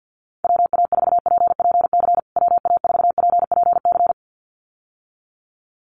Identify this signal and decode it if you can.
Morse code, heard as a keyed tone, about 50 wpm, 700 Hz. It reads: WA4PPC WA4PPC
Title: USA Call Signs Repeated 2x 50wpm